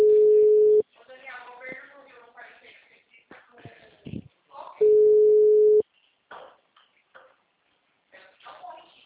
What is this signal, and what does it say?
Ringback tone (Europe, 425 Hz), with speech.